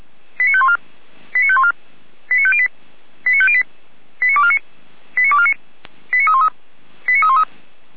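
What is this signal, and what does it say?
Five-tone selective call (ZVEI), heard as quick runs of five tones, tones of about 70 ms. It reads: 89514 89514 89508 89508 89249 89249 89313 89313
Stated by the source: ZVEI five-tone call received on SDR, 455 channel.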